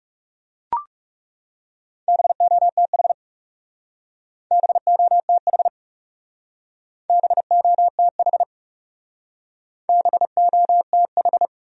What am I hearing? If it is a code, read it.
Morse code, heard as a keyed tone, about 36 wpm, 700 Hz. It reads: BOTH BOTH BOTH BOTH
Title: Speed-Racing Top 300 Words 30wpm